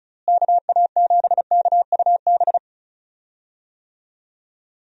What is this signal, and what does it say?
Morse code, heard as a keyed tone, about 36 wpm, 700 Hz. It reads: KA7KUB
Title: USA Call Signs - 6 Characters - Repeated 2x - 35wpm